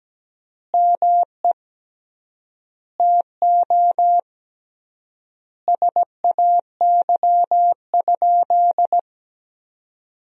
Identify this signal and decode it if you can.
Morse code, heard as a keyed tone, about 17 wpm, 700 Hz. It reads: ME TO SAY?
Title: 3x Word-Spacing - Sentences from Top 200 Words - 17wpm